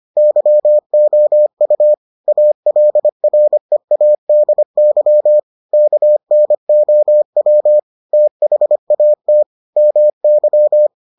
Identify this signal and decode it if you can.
Morse code, heard as a keyed tone, about 25 wpm, 600 Hz. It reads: YOU ALREADY KNOW THAT MY